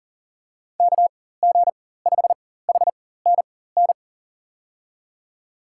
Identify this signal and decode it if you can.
Morse code, heard as a keyed tone, about 40 wpm, spaced as for 19 wpm, 700 Hz. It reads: KG5HNN
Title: USA Call Signs - 4x Character Spacing - 40wpm